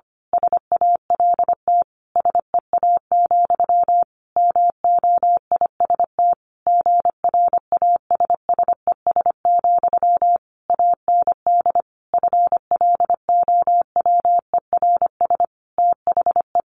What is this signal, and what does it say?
Morse code, heard as a keyed tone, about 25 wpm, 700 Hz. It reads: SALT SEA, MOIST GRASSES, AND FLOWERS THE
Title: A Sound Of Thunder 25wpm